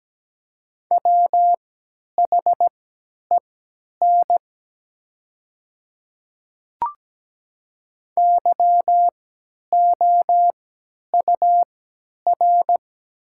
Morse code, heard as a keyed tone, about 17 wpm, spaced as for 10 wpm, 700 Hz. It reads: WHEN YOUR
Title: 3x Character Spacing - Top 100 Words - 17wpm